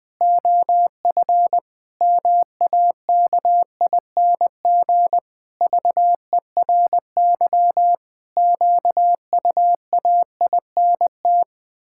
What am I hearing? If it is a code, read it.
Morse code, heard as a keyed tone, about 20 wpm, 700 Hz. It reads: OF MAKING VERY QUAINT
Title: Pangram Challenge v2 - 20wpm